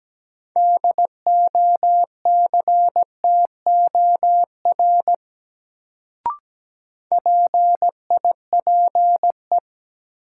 Morse code, heard as a keyed tone, about 17 wpm, 700 Hz. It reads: DOCTOR PIPE
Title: Sentences from Top 1500 Words - New Word Drill Alt - 17wpm